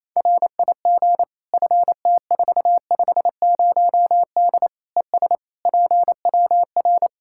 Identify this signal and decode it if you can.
Morse code, heard as a keyed tone, about 28 wpm, 700 Hz. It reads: RIG FT450D ES PWR
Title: Practice QSO - Exchange 2 - Info - 28wpm